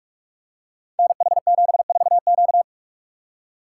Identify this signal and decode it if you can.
Morse code, heard as a keyed tone, about 44 wpm, 690 Hz. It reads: NH74Q
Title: Code Groups - 5 Letter:Numbers - 45wpm